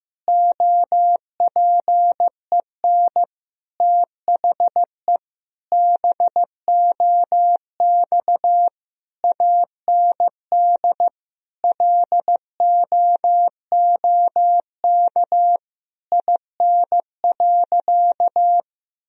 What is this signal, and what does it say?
Morse code, heard as a keyed tone, about 15 wpm, 700 Hz. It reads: OPEN THE BOX AND LOOK IN.
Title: Sentences from Top 1000 Words - Review Part 2 - 15wpm